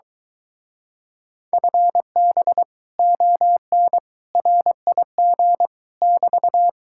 Morse code, heard as a keyed tone, about 23 wpm, 700 Hz. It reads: FB ON RIG <BT>